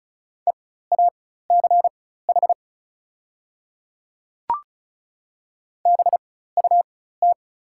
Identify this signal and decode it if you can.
Morse code, heard as a keyed tone, about 36 wpm, spaced as for 16 wpm, 700 Hz. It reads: EACH BUT
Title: 4x Character Spacing - Top 200 Words - 35wpm